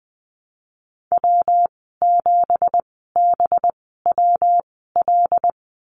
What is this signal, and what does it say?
Morse code, heard as a keyed tone, about 20 wpm, spaced as for 15 wpm, 700 Hz. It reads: W7BWL
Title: 193 Lesson 31g - Letter v - Call Signs 20wpm